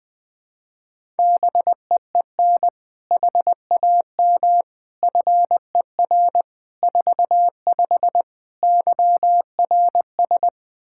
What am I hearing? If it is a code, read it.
Morse code, heard as a keyed tone, about 20 wpm, 700 Hz. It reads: BEEN HAM FER 45 YRS
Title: Practice QSO - Exchange 3 - More Info - 20wpm